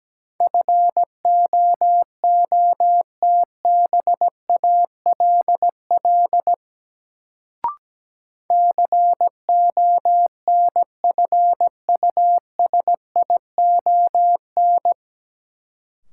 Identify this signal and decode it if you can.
Morse code, heard as a keyed tone, about 17 wpm, 700 Hz. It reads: FOOTBALL CONFUSION
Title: Sentences from Top 1600 Words - New Word Drill - 17wpm